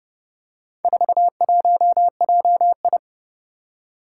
Morse code, heard as a keyed tone, about 30 wpm, 700 Hz. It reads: 41JI
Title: Four Challenging Letter-Number Combinations 30wpm